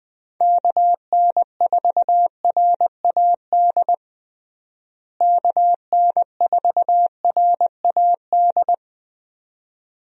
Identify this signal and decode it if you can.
Morse code, heard as a keyed tone, about 20 wpm, 700 Hz. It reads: KN4RAD KN4RAD
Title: USA Call Signs - 6 Characters - Repeated 2x - 20wpm